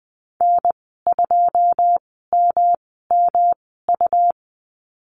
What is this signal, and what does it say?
Morse code, heard as a keyed tone, about 20 wpm, spaced as for 15 wpm, 700 Hz. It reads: N2MMU
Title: 147 Lesson 24g - Letter y - Call Signs 20wpm